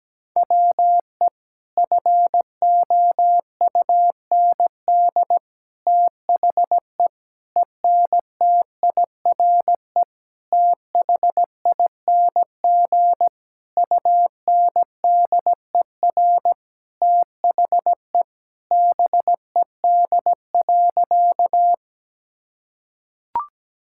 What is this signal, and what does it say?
Morse code, heard as a keyed tone, about 17 wpm, 700 Hz. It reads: WE FOUND THE ENTIRE THING UNDER THE BED.
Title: Sentences from Top 1000 Words - Review Part 5 - 17wpm